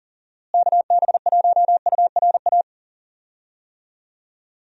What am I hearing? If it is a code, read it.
Morse code, heard as a keyed tone, about 40 wpm, 700 Hz. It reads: KB1URA
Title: USA Call Signs - 6 Characters - Repeated 3x - 40wpm